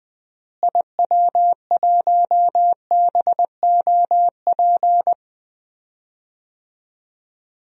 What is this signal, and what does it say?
Morse code, heard as a keyed tone, about 20 wpm, 700 Hz. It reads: IW1BOP
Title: DXCC-100 Call Signs - 6 Characters - Repeated 2x - 20wpm